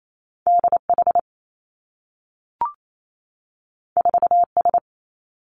Morse code, heard as a keyed tone, about 28 wpm, 700 Hz. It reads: D H 4 S
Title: Two Challenging Letter-Number Combinations 28wpm